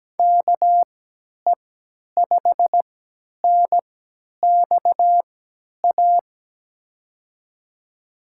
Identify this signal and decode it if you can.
Morse code, heard as a keyed tone, about 17 wpm, spaced as for 10 wpm, 700 Hz. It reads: KE5NXA
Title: USA Call Signs - 6 Characters - 3x Character Spacing - 17wpm